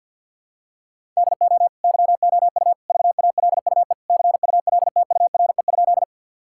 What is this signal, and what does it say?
Morse code, heard as a keyed tone, about 51 wpm, 700 Hz. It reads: DO YOU VALUE CULTURE?
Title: Sentences from Top 1000 Words - Review Part 6 - 50wpm